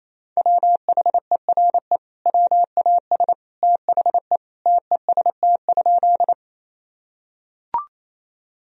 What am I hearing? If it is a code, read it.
Morse code, heard as a keyed tone, about 28 wpm, 700 Hz. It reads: WHERE WAS THE TEST?
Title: Sentences from Top 1000 Words - Review Part 3 - 28wpm